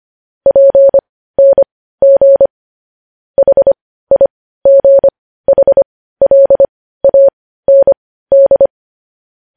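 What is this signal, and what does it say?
Morse code, heard as a keyed tone, about 25 wpm, spaced as for 15 wpm, 550 Hz. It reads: PNG HIGHLAND
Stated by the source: Morse code news headlines sent at 15 words per minute.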